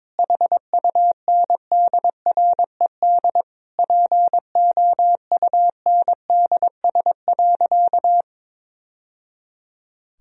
Morse code, heard as a keyed tone, about 22 wpm, 700 Hz. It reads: HUNDRED POUNDS.
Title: Sentences from Top 1000 Words - Review Part 2 - 22wpm